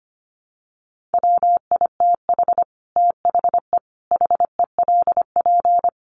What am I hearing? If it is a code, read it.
Morse code, heard as a keyed tone, about 25 wpm, 700 Hz. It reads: WITH THE HELP
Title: Sets of 5 Words 25wpm